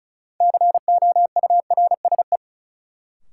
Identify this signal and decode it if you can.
Morse code, heard as a keyed tone, about 34 wpm, 700 Hz. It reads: COURSE
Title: Sentences from Top 500 Words - New Word Drill - 35wpm